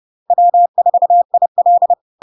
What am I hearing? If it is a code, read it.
Morse code, heard as a keyed tone, about 30 wpm, 700 Hz. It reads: W4IL
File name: LARK-FD-2022-Log-in-CW-30wpm-700-Hz